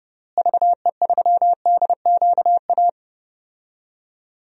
Morse code, heard as a keyed tone, about 32 wpm, 700 Hz. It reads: VE3DQA
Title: DXCC-100 Call Signs - 6 Characters - Repeated 3x - 30wpm